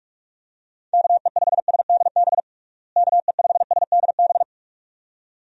Morse code, heard as a keyed tone, about 46 wpm, 700 Hz. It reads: KE5SDB KE5SDB
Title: USA Call Signs - 6 Characters - Repeated 2x - 45wpm